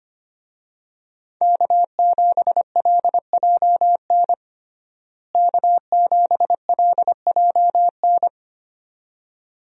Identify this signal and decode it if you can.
Morse code, heard as a keyed tone, about 25 wpm, 700 Hz. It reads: K7LJN K7LJN
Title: USA Call Signs - 5 Characters - Repeated 2x - 25wpm